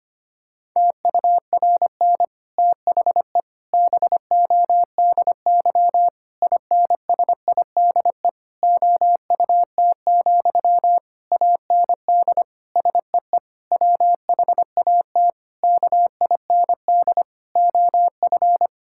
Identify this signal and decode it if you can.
Morse code, heard as a keyed tone, about 25 wpm, 700 Hz. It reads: TURN THE BODY INSIDE OUT, AND SEE WHAT KIND OF
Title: Sentences from Top 1000 Words - Review Part 6 - 25wpm